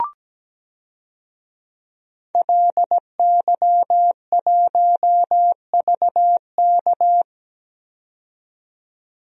Morse code, heard as a keyed tone, about 17 wpm, 700 Hz. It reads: LY1VK
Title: DXCC-100 Call Signs - 5 Characters - 17wpm